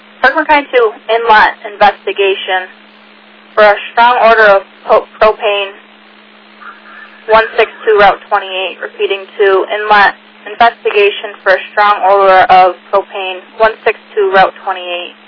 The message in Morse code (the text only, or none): none